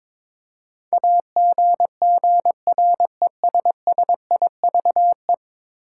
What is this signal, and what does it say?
Morse code, heard as a keyed tone, about 22 wpm, 700 Hz. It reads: AGGRESSIVE
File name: practice / 22wpm / 10 Letter Words - First 3 Letters Given - 22wpm